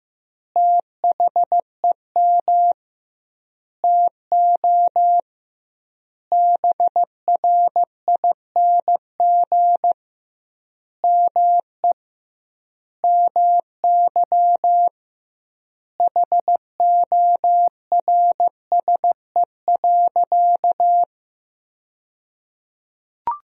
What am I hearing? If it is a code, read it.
Morse code, heard as a keyed tone, about 15 wpm, 700 Hz. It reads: THEM TO BRING ME MY HORSE.
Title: 2x Word-Spacing - Sentences from Top 500 Words - 15wpm